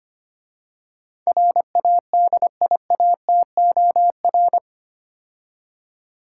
Morse code, heard as a keyed tone, about 28 wpm, 700 Hz. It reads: RADIATOR
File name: practice / 25wpm / Difficult Words - Speed-Racing with Hints - 25wpm